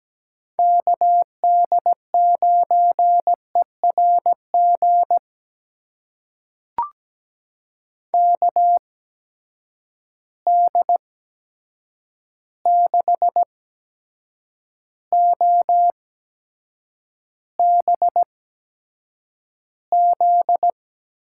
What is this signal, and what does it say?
Morse code, heard as a keyed tone, about 17 wpm, spaced as for 7 wpm, 700 Hz. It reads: KD9ERG KD6OBZ
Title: USA Call Signs - 6 Characters - 8x Character Spacing - 17wpm